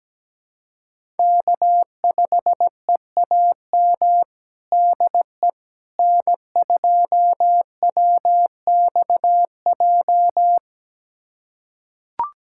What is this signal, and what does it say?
Morse code, heard as a keyed tone, about 17 wpm, 700 Hz. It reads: K5EAM DE N2WXJ
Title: USA Callsign DE Callsign - 5 Characters - 17wpm